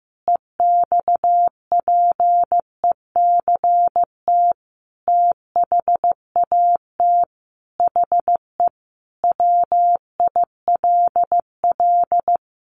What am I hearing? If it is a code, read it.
Morse code, heard as a keyed tone, about 15 wpm, 700 Hz. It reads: EXPECT THAT HE WILL
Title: Sentences from Top 1100 Words 15wpm